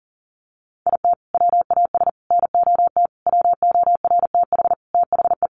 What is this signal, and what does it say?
Morse code, heard as a keyed tone, about 40 wpm, 700 Hz. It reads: IT WAS NOT WORTH THE
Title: Sentences from Top 1200 Words 40wpm